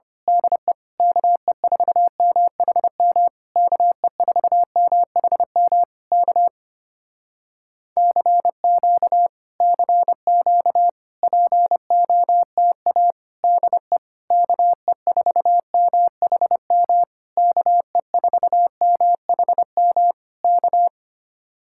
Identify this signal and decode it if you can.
Morse code, heard as a keyed tone, about 25 wpm, 700 Hz. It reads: DE KE4MHM KE4MHM K CQ CQ POTA DE KE4MHM KE4MHM K
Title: Speed-Racing Common QSO Phrases - 25wpm